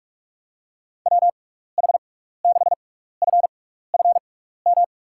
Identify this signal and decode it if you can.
Morse code, heard as a keyed tone, about 43 wpm, spaced as for 15 wpm, 700 Hz. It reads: WH6FFM
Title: USA Call Signs - 6 Characters - 6x Character Spacing - 45wpm